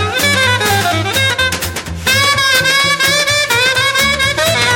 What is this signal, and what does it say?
Emergency Alert System (SAME) tones, heard as harsh buzzing bursts. It is heard from 1.1 to 4.0 s.